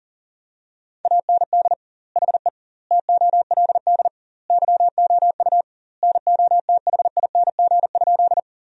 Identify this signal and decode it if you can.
Morse code, heard as a keyed tone, about 37 wpm, 700 Hz. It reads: AND HE TOLD YOU NOTHING?
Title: 2x Word-Spacing - Sentences from Top 500 Words - 40wpm